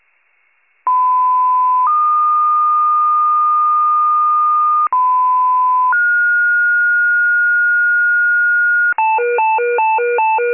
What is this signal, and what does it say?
Two-tone page, heard as a one-second tone, then a three-second tone, about 1010 then 1220 Hz, about 1010 then 1430 Hz, with noise. It is heard from 0.9 to 8.9 s.